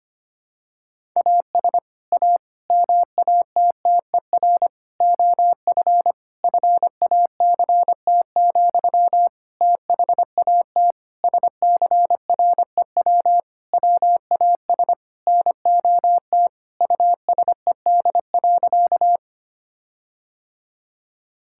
Morse code, heard as a keyed tone, about 25 wpm, 700 Hz. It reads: AS A MATTER OF FACT, THAT SCREW WAS NOT USED.
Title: Sentences from Top 1600 Words 25wpm